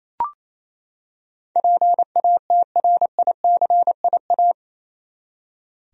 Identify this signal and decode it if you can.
Morse code, heard as a keyed tone, about 28 wpm, 700 Hz. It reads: PATRICIA
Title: USA Common Names - 28wpm